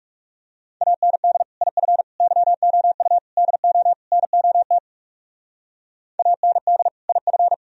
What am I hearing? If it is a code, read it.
Morse code, heard as a keyed tone, about 37 wpm, 700 Hz. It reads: AND IF YOU DO NOT AND IF
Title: Speed-Racing - Sets of 5 Words 30wpm